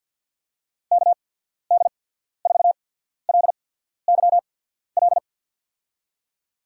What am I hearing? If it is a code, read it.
Morse code, heard as a keyed tone, about 50 wpm, spaced as for 14 wpm, 700 Hz. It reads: KD4LYL
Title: USA Call Signs - 6 Characters - 8x Character Spacing - 50wpm